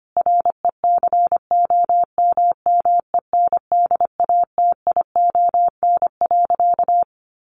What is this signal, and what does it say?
Morse code, heard as a keyed tone, about 25 wpm, 700 Hz. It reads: RECOMMENDATION.
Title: Sentences from Top 1400 Words 25wpm